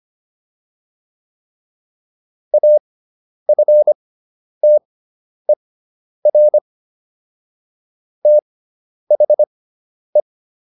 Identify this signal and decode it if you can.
Morse code, heard as a keyed tone, about 25 wpm, spaced as for 10 wpm, 600 Hz. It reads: AFTER THE